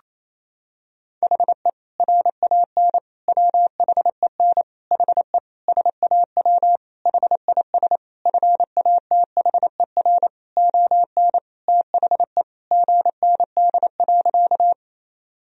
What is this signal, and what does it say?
Morse code, heard as a keyed tone, about 28 wpm, 700 Hz. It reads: HE RAN WHEN HE SAW HIS FATHER ON THE GND.
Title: Sentences from Top 500 Words and Abbreviations 28wpm